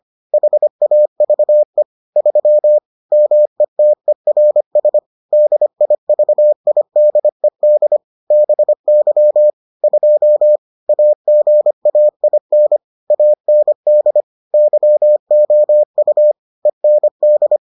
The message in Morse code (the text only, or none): HAVE 3 METERS DIVIDED BY 2 AGAIN AND YOU END